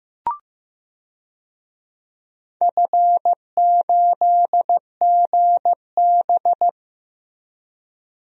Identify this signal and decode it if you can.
Morse code, heard as a keyed tone, about 15 wpm, 700 Hz. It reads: F8GB
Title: DXCC-100 Call Signs - 4 Characters - Repeated 3x - 15wpm